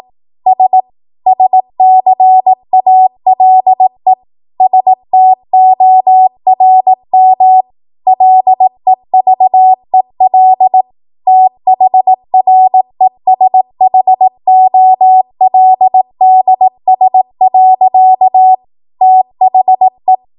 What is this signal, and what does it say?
Morse code, heard as a keyed tone, about 18 wpm, 750 Hz. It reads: S SCALE STORM LEVEL THRESHOLDS. THE
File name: Morse / 18WPM